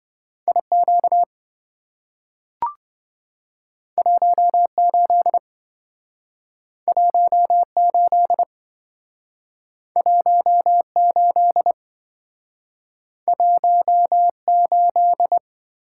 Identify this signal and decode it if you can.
Morse code, heard as a keyed tone, about 27 wpm, 700 Hz. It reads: IQ 18 18 18 18